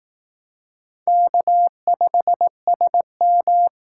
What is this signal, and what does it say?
Morse code, heard as a keyed tone, about 19 wpm, 700 Hz. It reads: K5SM